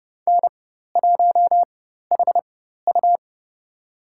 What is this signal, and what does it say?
Morse code, heard as a keyed tone, about 30 wpm, spaced as for 14 wpm, 700 Hz. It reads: N1HU
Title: USA Call Signs - 4x Character Spacing - 30wpm